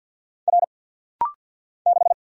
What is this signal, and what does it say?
Morse code, heard as a keyed tone, about 50 wpm, 700 Hz. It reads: R 6
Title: Single Letter-Number - Rapid-Fire 50wpm